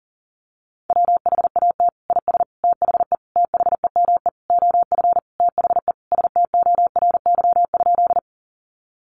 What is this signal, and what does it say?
Morse code, heard as a keyed tone, about 40 wpm, 700 Hz. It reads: WHAT IS THE THEME OF THE STORY?
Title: Sentences from Top 1100 Words 40wpm